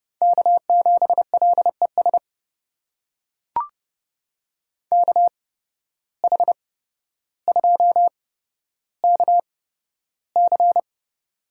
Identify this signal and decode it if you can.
Morse code, heard as a keyed tone, about 30 wpm, spaced as for 13 wpm, 700 Hz. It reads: K7LES KH2KC
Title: USA Call Signs - 5 Characters - 8x Character Spacing - 30wpm